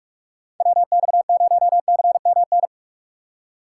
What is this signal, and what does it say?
Morse code, heard as a keyed tone, about 42 wpm, 700 Hz. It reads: WX0CMN
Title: USA Call Signs - 6 Characters - 4x Character Spacing - 45wpm